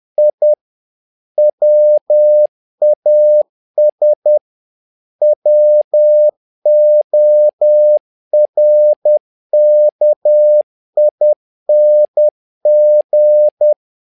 Morse code, heard as a keyed tone, about 10 wpm, 600 Hz. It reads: I WAS WORKING
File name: learning center / tower trouble / tower trouble 10wpm